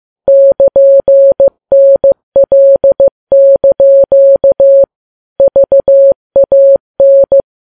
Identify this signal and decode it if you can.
Morse code, heard as a keyed tone, about 15 wpm, 550 Hz. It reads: <KN>NL) VAN
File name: Files Web tls / Morse on9fv / morse15kps